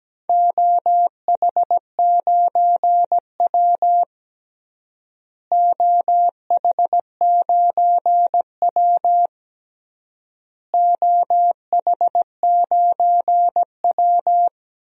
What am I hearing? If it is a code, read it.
Morse code, heard as a keyed tone, about 17 wpm, 700 Hz. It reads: OH9W OH9W OH9W
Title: DXCC-100 Call Signs - 4 Characters - Repeated 3x - 17wpm